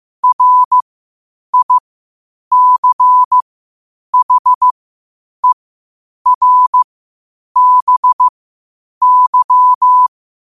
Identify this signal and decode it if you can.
Morse code, heard as a keyed tone, about 15 wpm, spaced as for 8 wpm, 1000 Hz. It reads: RICHERBY
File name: qotdfiles / 8wpm